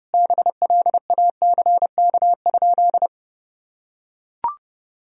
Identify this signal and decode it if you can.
Morse code, heard as a keyed tone, about 30 wpm, 700 Hz. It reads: BLACK?
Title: Sentences from Top 400 Words and Abbreviations 30wpm